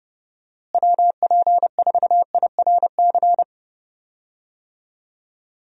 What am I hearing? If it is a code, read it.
Morse code, heard as a keyed tone, about 30 wpm, 700 Hz. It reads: WP4IRC
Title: USA Call Signs - 6 Characters - Repeated 3x - 30wpm